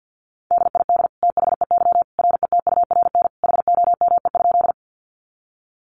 Morse code, heard as a keyed tone, about 50 wpm, 700 Hz. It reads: DID THEY RETURN HOME?
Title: Sentences from Top 800 Words 50wpm